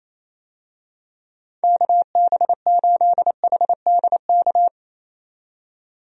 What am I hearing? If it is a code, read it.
Morse code, heard as a keyed tone, about 28 wpm, 700 Hz. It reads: KB8HDK
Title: USA Call Signs - 6 Characters - 28wpm